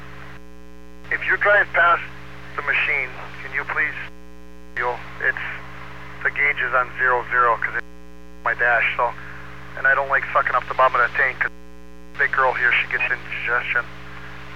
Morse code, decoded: none

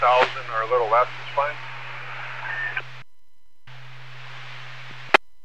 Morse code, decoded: none

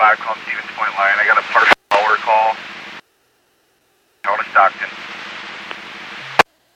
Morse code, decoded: none

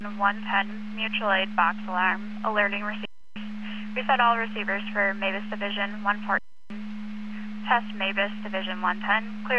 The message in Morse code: none

